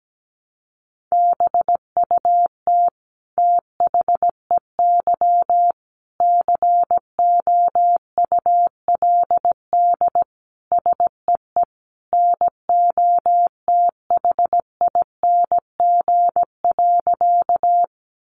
Morse code, heard as a keyed tone, about 17 wpm, 700 Hz. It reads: BUT THEY COULD SEE NOTHING.
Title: Sentences from Top 500 Words 17wpm